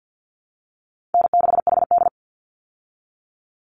Morse code, heard as a keyed tone, about 50 wpm, 700 Hz. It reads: N6HD
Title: Call Signs Repeated 2x - 50wpm